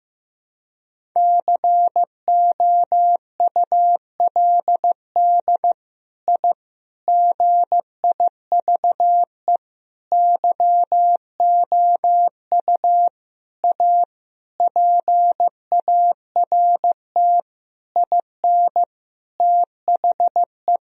Morse code, heard as a keyed tone, about 15 wpm, 700 Hz. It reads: COULD I GIVE YOU A PART IN THE
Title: Sentences from Top 1000 Words - Review Part 1 - 15wpm